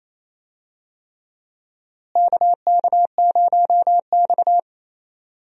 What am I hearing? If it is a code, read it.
Morse code, heard as a keyed tone, about 28 wpm, 700 Hz. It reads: KK0X
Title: USA Call Signs - 4 Characters - Repeated 3x - 28wpm